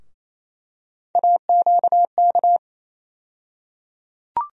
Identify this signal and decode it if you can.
Morse code, heard as a keyed tone, about 28 wpm, 700 Hz. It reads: AQK
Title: USA Call Sign Postfixes - 3 Letters 28wpm